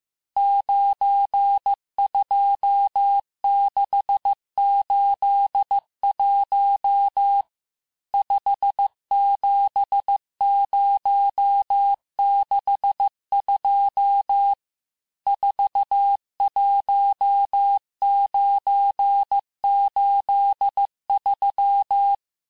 Morse code, heard as a keyed tone, about 15 wpm, 800 Hz. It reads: 92681 57062 41983
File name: cours-du-lundi files / cours-1585-texte-chiffre-vitesse-14mpm